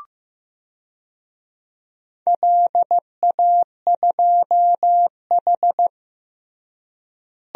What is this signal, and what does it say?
Morse code, heard as a keyed tone, about 15 wpm, 700 Hz. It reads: LA2H
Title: DXCC-100 Call Signs - 4 Characters - Repeated 2x - 15wpm